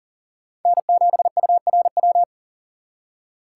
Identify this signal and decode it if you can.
Morse code, heard as a keyed tone, about 51 wpm, 700 Hz. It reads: N7URW